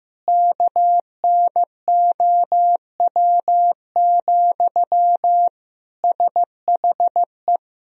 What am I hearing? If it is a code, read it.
Morse code, heard as a keyed tone, about 15 wpm, 700 Hz. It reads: KNOW, SHE
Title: Sentences from Top 1000 Words - Review Part 3 - 15wpm